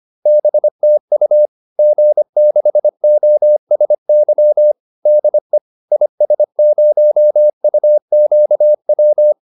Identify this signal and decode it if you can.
Morse code, heard as a keyed tone, about 25 wpm, 600 Hz. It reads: BTU G6OSY DE IS0UQW
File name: QSO409 25